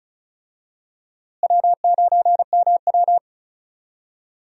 Morse code, heard as a keyed tone, about 36 wpm, 700 Hz. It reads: W9MW
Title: USA Call Signs - 4 Characters - Repeated 2x - 35wpm